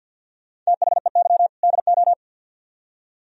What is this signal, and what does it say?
Morse code, heard as a keyed tone, about 61 wpm, 700 Hz. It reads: THEY DO
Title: Speed-Racing - Sets of 2 Words 50wpm